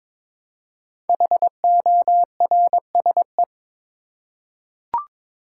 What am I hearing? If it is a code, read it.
Morse code, heard as a keyed tone, about 22 wpm, 700 Hz. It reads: HORSE
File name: 5 Letter Words - First 3 Letters Given - 22wpm